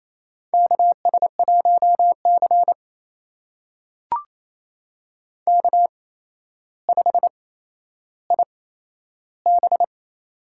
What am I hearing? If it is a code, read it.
Morse code, heard as a keyed tone, about 28 wpm, spaced as for 12 wpm, 700 Hz. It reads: KS1C K5IB
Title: USA Call Signs - 4 Characters - 8x Character Spacing - 28wpm